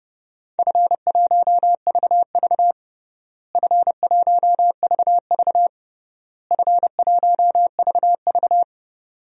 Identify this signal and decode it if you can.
Morse code, heard as a keyed tone, about 32 wpm, 700 Hz. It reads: F1VV F1VV F1VV
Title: DXCC-100 Call Signs - 4 Characters - Repeated 3x - 30wpm